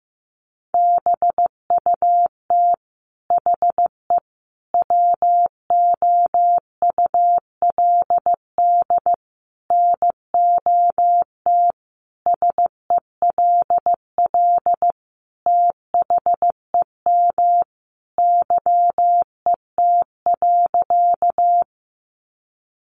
Morse code, heard as a keyed tone, about 15 wpm, 700 Hz. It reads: BUT HE WOULD NOT SELL THEM YET.
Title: Sentences from Top 1100 Words 15wpm